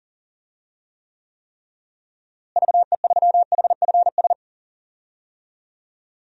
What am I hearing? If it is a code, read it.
Morse code, heard as a keyed tone, about 40 wpm, 700 Hz. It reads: VE3HFS
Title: DXCC-100 Call Signs - 6 Characters - Repeated 3x - 40wpm